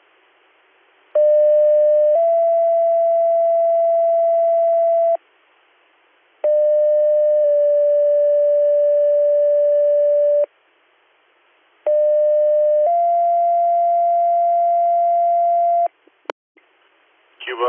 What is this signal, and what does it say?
Two-tone page, heard as a one-second tone, then a three-second tone, about 600 then 670 Hz, about 600 then 580 Hz, about 600 then 690 Hz. It is heard from 1.1 to 15.9 s.